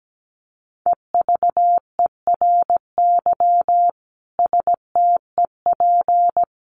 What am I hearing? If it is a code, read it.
Morse code, heard as a keyed tone, about 17 wpm, 700 Hz. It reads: EVERY STEP
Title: Sets of 5 Words Encore2 17wpm